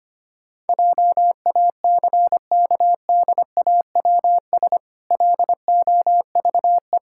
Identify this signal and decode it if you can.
Morse code, heard as a keyed tone, about 25 wpm, 700 Hz. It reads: JACKDAWS LOVE
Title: Pangram Challenge v2 - 25wpm